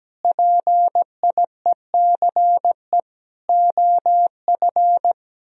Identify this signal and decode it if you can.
Morse code, heard as a keyed tone, about 17 wpm, 700 Hz. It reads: PIECE OF
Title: Sentences from Top 1000 Words - Review Part 2 - 17wpm